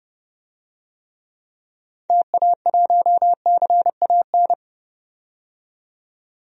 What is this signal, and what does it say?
Morse code, heard as a keyed tone, about 30 wpm, 700 Hz. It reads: TA1CAN
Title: DXCC-100 Call Signs - 6 Characters - 30wpm